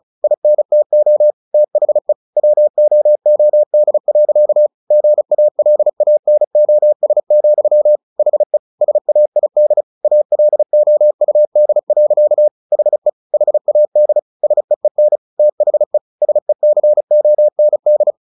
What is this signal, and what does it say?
Morse code, heard as a keyed tone, about 34 wpm, 600 Hz. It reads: INTO THE WOOD. GALANOS, HE SAID ALOUD. HE HAD SEEN THE SECOND
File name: the old man and the sea 2 35wpm